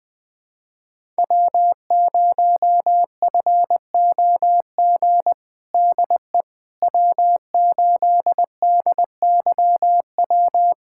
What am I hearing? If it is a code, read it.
Morse code, heard as a keyed tone, about 20 wpm, 700 Hz. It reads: W0FOG DE W8DYW
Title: USA Callsign DE Callsign - 5 Characters - 20wpm